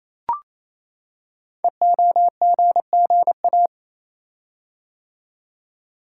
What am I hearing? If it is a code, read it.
Morse code, heard as a keyed tone, about 28 wpm, 700 Hz. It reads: EOGGA
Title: Code Groups - 5 Letters - 28wpm